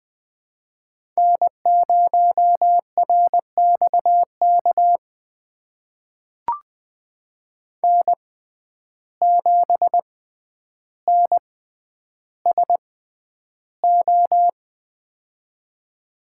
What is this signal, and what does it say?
Morse code, heard as a keyed tone, about 20 wpm, spaced as for 10 wpm, 700 Hz. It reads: N0RXK N7NSO
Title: USA Call Signs - 5 Characters - 6x Character Spacing - 20wpm